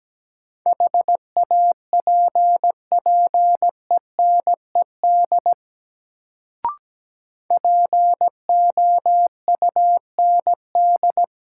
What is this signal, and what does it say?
Morse code, heard as a keyed tone, about 17 wpm, 700 Hz. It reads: HAPPENED POUND
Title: Sentences from Top 1000 Words - Review Part 2 - New Word Drill - 17wpm